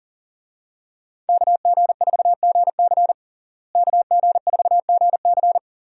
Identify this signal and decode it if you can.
Morse code, heard as a keyed tone, about 40 wpm, 700 Hz. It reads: KG4GC KG4GC
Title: USA Call Signs - 5 Characters - Repeated 3x - 40wpm